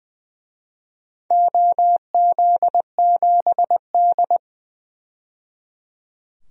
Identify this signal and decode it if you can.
Morse code, heard as a keyed tone, about 20 wpm, 700 Hz. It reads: OZ7D
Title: DXCC-100 Call Signs - 4 Characters - 20wpm